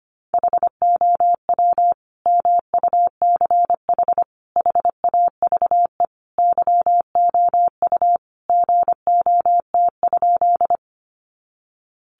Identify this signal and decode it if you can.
Morse code, heard as a keyed tone, about 25 wpm, 700 Hz. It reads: HOW MUCH HAVE YOU GOT?
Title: Sentences from Top 300 Words 25wpm